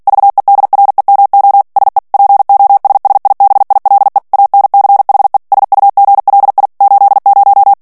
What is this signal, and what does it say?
Morse code, heard as a keyed tone, about 48 wpm, 800 Hz. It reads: VEDREMO SE POSSIBILE ANCHE SUGLI 80
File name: morse mp3 / ITALIEN texte clair 1 vitesse 45 mots minute